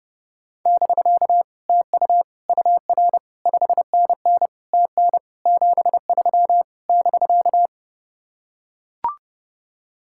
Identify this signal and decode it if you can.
Morse code, heard as a keyed tone, about 30 wpm, 700 Hz. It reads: <BK> TU UR 5NN TN 73 <BK>